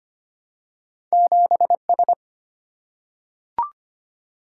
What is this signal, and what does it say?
Morse code, heard as a keyed tone, about 25 wpm, 700 Hz. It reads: 7S
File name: Two Letter-Number Combinations - 25wpm